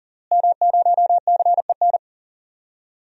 Morse code, heard as a keyed tone, about 37 wpm, 700 Hz. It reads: M0CEN
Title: DXCC-100 Call Signs - 5 Characters - Repeated 2x - 40wpm